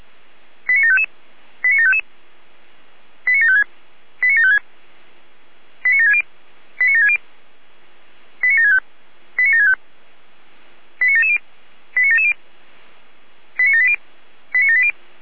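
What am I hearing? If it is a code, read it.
Five-tone selective call (ZVEI), heard as quick runs of five tones, tones of about 70 ms. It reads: 89755 89755 89756 89756 89760 89760 89765 89765 89770 89770 89780 89780
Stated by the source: ZVEI five-tone call for address 89740, received by SDR.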